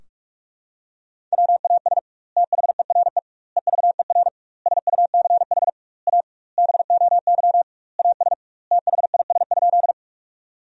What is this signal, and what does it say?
Morse code, heard as a keyed tone, about 43 wpm, 700 Hz. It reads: WAS THERE EVER SUCH A BOY AS THIS?
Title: 2x Word-Spacing - Sentences from Top 400 Words - 45wpm